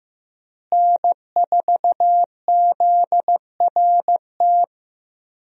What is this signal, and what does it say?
Morse code, heard as a keyed tone, about 15 wpm, 700 Hz. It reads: N4ZRT
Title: USA Call Signs - 5 Characters - Repeated 3x - 15wpm